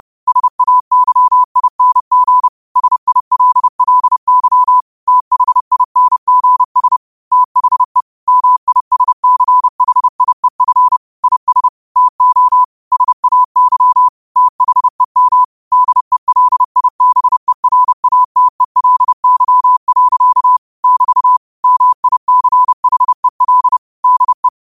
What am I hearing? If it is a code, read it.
Morse code, heard as a keyed tone, about 30 wpm, 1000 Hz. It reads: SAYING SILLY THINGS THE MISCHIEF IS TO SAY THEM DELIBERATELY. <BT> MICHEL DE